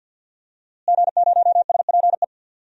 Morse code, heard as a keyed tone, about 47 wpm, 700 Hz. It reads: G0SPE
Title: DXCC-100 Call Signs - 5 Characters - Repeated 3x - 50wpm